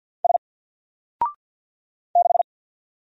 Morse code, heard as a keyed tone, about 50 wpm, 700 Hz. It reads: S 6
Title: Single Character (Letter:Number:Punctuation:Prosign) 50wpm